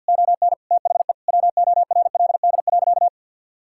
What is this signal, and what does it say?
Morse code, heard as a keyed tone, about 50 wpm, 700 Hz. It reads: ON THE WORLD.
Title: Sentences from Top 1000 Words - Review Part 6 - 50wpm